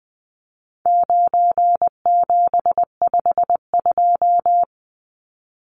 Morse code, heard as a keyed tone, about 20 wpm, 700 Hz. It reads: 9752
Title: Numbers - 4-Digit 20wpm